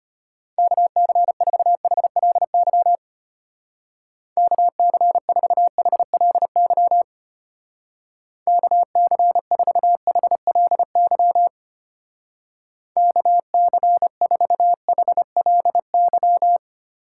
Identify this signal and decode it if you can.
Morse code, heard as a keyed tone, about 32 wpm, 700 Hz. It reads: KC4HLY KC4HLY KC4HLY KC4HLY